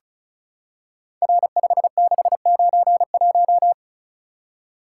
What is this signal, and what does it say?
Morse code, heard as a keyed tone, about 34 wpm, 700 Hz. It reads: R5691
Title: Code Groups - 5 Letter:Numbers - 35wpm